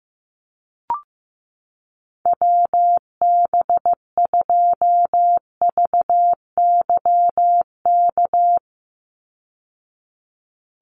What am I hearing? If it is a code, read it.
Morse code, heard as a keyed tone, about 15 wpm, 700 Hz. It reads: WB2VYK
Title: Call Signs Repeated 2x - 15wpm